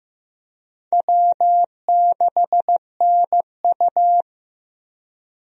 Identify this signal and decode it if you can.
Morse code, heard as a keyed tone, about 15 wpm, 700 Hz. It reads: W6NU
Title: USA Call Signs - 4 Characters - Repeated 3x - 15wpm